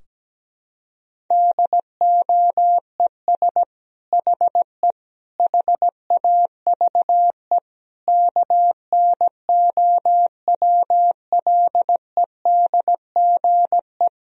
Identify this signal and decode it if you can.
Morse code, heard as a keyed tone, about 17 wpm, 700 Hz. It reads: DOES HE HAVE KNOWLEDGE
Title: Sentences from Top 1000 Words - Review Part 3 - 17wpm